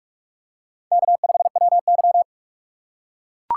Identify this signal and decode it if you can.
Morse code, heard as a keyed tone, about 44 wpm, 700 Hz. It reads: K5WY